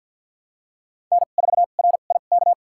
Morse code, heard as a keyed tone, about 51 wpm, spaced as for 36 wpm, 700 Hz. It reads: N4RIK
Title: USA Call Signs - 5 Characters - 2x Character Spacing - 50wpm